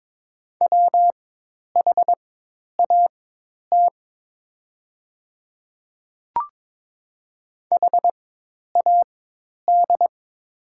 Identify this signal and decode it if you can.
Morse code, heard as a keyed tone, about 22 wpm, spaced as for 10 wpm, 700 Hz. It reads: WHAT HAD
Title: 4x Character Spacing - Top 100 Words - 22wpm